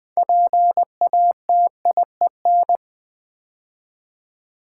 Morse code, heard as a keyed tone, about 20 wpm, 700 Hz. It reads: PATIEN
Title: Build A Word - 3 Letters At A Time - 8-Letter Words - 20wpm